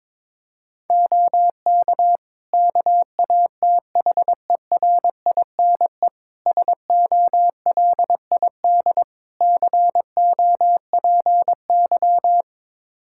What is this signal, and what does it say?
Morse code, heard as a keyed tone, about 22 wpm, 700 Hz. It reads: OK KATHERINE SOLID COPY
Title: Practice QSO - Exchange 3 - More Info - 22wpm